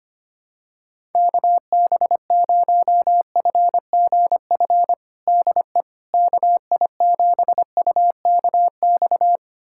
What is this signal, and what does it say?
Morse code, heard as a keyed tone, about 25 wpm, 700 Hz. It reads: KB0FGF DE KI7UKX
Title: USA Callsign DE Callsign - 6 Characters - 25wpm